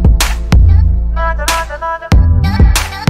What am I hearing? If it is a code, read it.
There is no code.